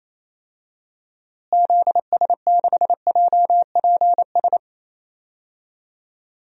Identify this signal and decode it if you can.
Morse code, heard as a keyed tone, about 27 wpm, 700 Hz. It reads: ZS6JPS
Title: DXCC-100 Call Signs - 6 Characters - Repeated 3x - 28wpm